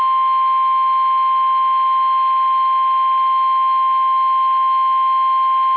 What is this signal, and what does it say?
Emergency Alert System (SAME) tones (the 1050 Hz attention tone). It is heard from 0.8 to 5.0 s.